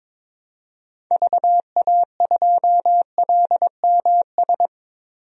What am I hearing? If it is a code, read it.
Morse code, heard as a keyed tone, about 22 wpm, 700 Hz. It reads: VA2LMS
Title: DXCC-100 Call Signs - 6 Characters - Repeated 2x - 22wpm